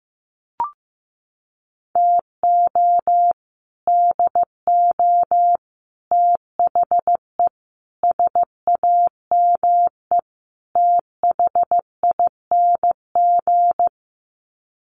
Morse code, heard as a keyed tone, about 15 wpm, 700 Hz. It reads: TO DO THE SAME THING
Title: Sets of 5 Words Encore 15wpm